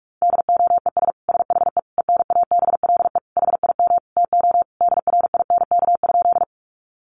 Morse code, heard as a keyed tone, about 44 wpm, 700 Hz. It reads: DOES SHE ENABLE HIM TO DRINK?
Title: Sentences from Top 1500 Words 45wpm